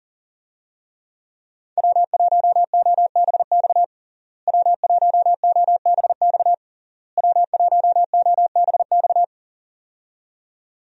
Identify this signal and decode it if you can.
Morse code, heard as a keyed tone, about 40 wpm, 700 Hz. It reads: W1OBX W1OBX W1OBX
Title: USA Call Signs - 5 Characters - Repeated 3x - 40wpm